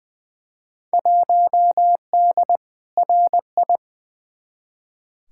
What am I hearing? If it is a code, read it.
Morse code, heard as a keyed tone, about 20 wpm, 700 Hz. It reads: 1D RI
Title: Field Day Exchanges - 20wpm